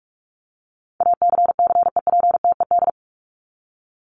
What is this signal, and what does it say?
Morse code, heard as a keyed tone, about 44 wpm, 700 Hz. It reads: ACCEPTED
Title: ED Suffix - Top 100 Words with Base Words 45wpm